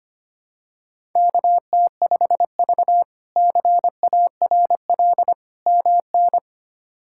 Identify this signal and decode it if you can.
Morse code, heard as a keyed tone, about 25 wpm, 700 Hz. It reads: KT5V CARL MN
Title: K1USN SST QSO Practice - 25wpm